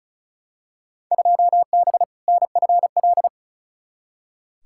Morse code, heard as a keyed tone, about 36 wpm, 690 Hz. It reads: 2B NFL
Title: Field Day Exchanges Encore - 35wpm